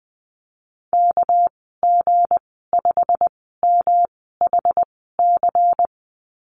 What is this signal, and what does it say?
Morse code, heard as a keyed tone, about 20 wpm, spaced as for 14 wpm, 700 Hz. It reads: KG5MHC
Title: 207 Lesson 33g - Letter j - Call Signs 20wpm